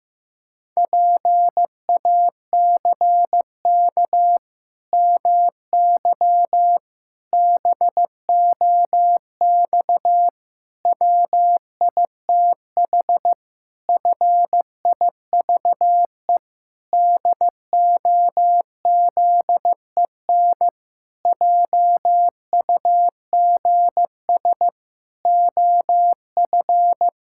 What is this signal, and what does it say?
Morse code, heard as a keyed tone, about 16 wpm, 700 Hz. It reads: PACK MY BOX WITH FIVE DOZEN JUGS OF
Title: Pangram Challenge v2 - 15wpm